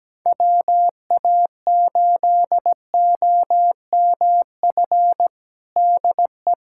Morse code, heard as a keyed tone, about 17 wpm, 700 Hz. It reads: WA8OMF DE
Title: USA Callsign DE Callsign - 6 Characters - 17wpm